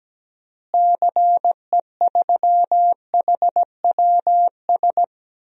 Morse code, heard as a keyed tone, about 17 wpm, 700 Hz. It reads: CE3HWS
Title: DXCC-100 Call Signs - 6 Characters - Repeated 3x - 17wpm